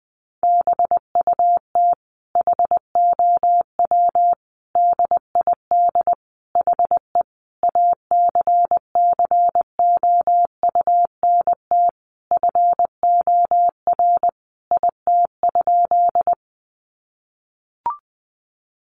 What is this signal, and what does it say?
Morse code, heard as a keyed tone, about 20 wpm, 700 Hz. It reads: BUT HOW DID HE ACCOUNT FOR IT?
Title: Sentences from Top 900 Words 20wpm